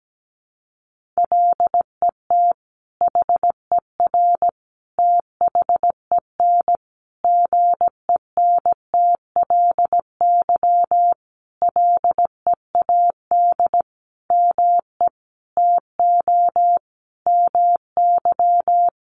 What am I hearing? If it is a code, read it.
Morse code, heard as a keyed tone, about 17 wpm, 700 Hz. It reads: LET HER THEN GENTLY LEAD ME TO MY
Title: Sentences from Top 1500 Words 17wpm